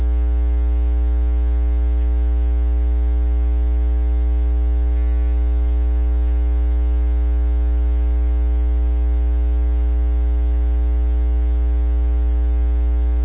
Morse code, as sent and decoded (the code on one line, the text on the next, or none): none